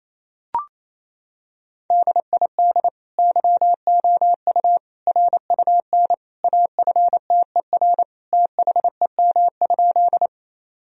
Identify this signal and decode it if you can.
Morse code, heard as a keyed tone, about 28 wpm, 700 Hz. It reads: DID YOU RUN AFTER THEM?
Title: Sentences from Top 1000 Words - Review Part 1 - 28wpm